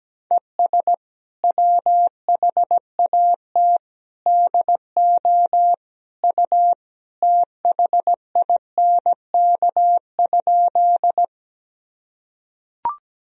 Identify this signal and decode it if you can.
Morse code, heard as a keyed tone, about 17 wpm, 700 Hz. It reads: ES WHAT DO U THINK?
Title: Sentences from Top 200 Words and Abbreviations 17wpm